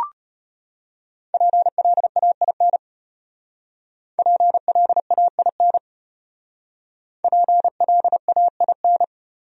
Morse code, heard as a keyed tone, about 34 wpm, 700 Hz. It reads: PLAIN PLAIN PLAIN